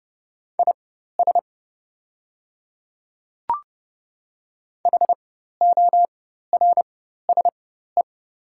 Morse code, heard as a keyed tone, about 31 wpm, spaced as for 14 wpm, 700 Hz. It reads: IS HORSE
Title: 4x Character Spacing - Top 300 Words - 30wpm